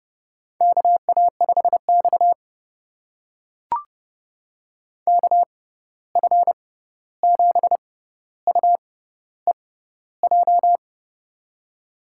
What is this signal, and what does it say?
Morse code, heard as a keyed tone, about 30 wpm, spaced as for 14 wpm, 700 Hz. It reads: KA5X KF7UEJ
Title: USA Call Signs - 6x Character Spacing - 30wpm